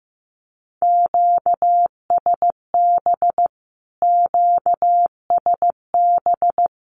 Morse code, heard as a keyed tone, about 15 wpm, 700 Hz. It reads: QSB QSB
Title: Q-Codes Repeated 3x 15wpm